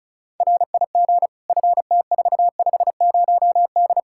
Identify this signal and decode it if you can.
Morse code, heard as a keyed tone, about 36 wpm, 700 Hz. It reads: RIG FT450D
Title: Practice QSO - Exchange 2 - Info - 35wpm